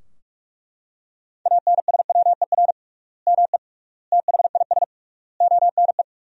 Morse code, heard as a keyed tone, about 41 wpm, 700 Hz. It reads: ANSWER ME THIS ONE
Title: 3x Word-Spacing - Sentences from Top 400 Words - 45wpm